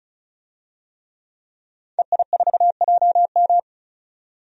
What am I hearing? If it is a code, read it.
Morse code, heard as a keyed tone, about 34 wpm, 700 Hz. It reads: EI4JM
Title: DXCC-100 Call Signs - 5 Characters - 35wpm